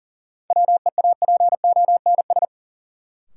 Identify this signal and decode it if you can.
Morse code, heard as a keyed tone, about 40 wpm, 700 Hz. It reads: WEAPONS
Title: Sets of 5 Words Encore - New Word Drill - 40wpm